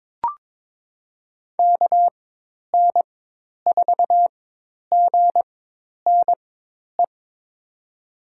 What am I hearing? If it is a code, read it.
Morse code, heard as a keyed tone, about 22 wpm, spaced as for 10 wpm, 700 Hz. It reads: KN4GNE
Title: USA Call Signs - 6 Characters - 4x Character Spacing - 22wpm